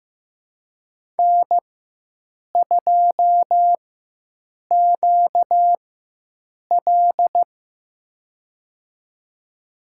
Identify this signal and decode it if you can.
Morse code, heard as a keyed tone, about 15 wpm, spaced as for 7 wpm, 700 Hz. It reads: N2QL
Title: USA Call Signs - 4 Characters - 4x Character Spacing - 15wpm